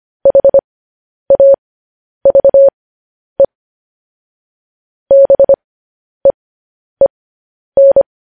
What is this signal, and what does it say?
Morse code, heard as a keyed tone, about 25 wpm, spaced as for 10 wpm, 550 Hz. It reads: HAVE BEEN
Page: news headlines in morse code at 10 wpm